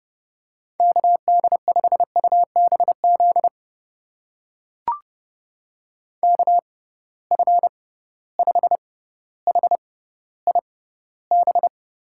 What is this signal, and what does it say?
Morse code, heard as a keyed tone, about 32 wpm, spaced as for 16 wpm, 700 Hz. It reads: KD5UBZ KF5HIB
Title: USA Call Signs - 6 Characters - 6x Character Spacing - 30wpm